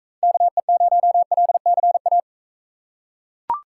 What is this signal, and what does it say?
Morse code, heard as a keyed tone, about 40 wpm, 700 Hz. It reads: KE0LCA